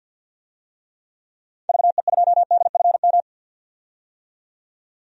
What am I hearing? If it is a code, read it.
Morse code, heard as a keyed tone, about 51 wpm, 700 Hz. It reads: VE2DFM
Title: DXCC-100 Call Signs - 6 Characters - 50wpm